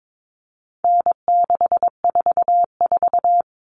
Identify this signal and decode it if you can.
Morse code, heard as a keyed tone, about 22 wpm, 700 Hz. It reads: N644
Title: All Cut Numbers - 4-Digit 22wpm